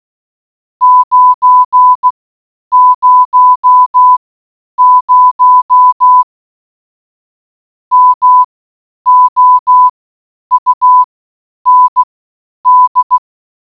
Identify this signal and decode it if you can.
Morse code, heard as a keyed tone, about 16 wpm, spaced as for 9 wpm, 1000 Hz. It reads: 900 MOUND